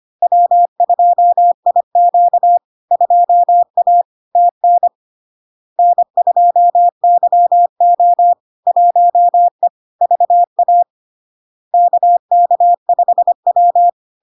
Morse code, heard as a keyed tone, about 25 wpm, 700 Hz. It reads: W2IQ 2A TN N2YO 1E VA KK5W
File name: LARK-FD-2022-Log-in-CW-25wpm-700-Hz